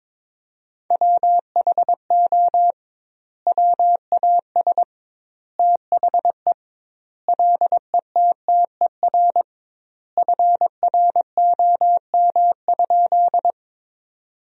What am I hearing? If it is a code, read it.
Morse code, heard as a keyed tone, about 22 wpm, 700 Hz. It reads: WHO WAS THE LETTER FROM?
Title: 2x Word-Spacing - Sentences from Top 300 Words - 22wpm